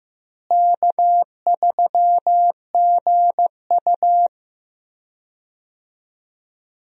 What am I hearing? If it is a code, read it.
Morse code, heard as a keyed tone, about 15 wpm, 700 Hz. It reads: K3GU
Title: USA Call Signs - 4 Characters - Repeated 3x - 15wpm